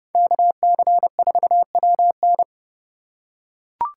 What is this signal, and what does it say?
Morse code, heard as a keyed tone, about 29 wpm, 700 Hz. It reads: KC4WN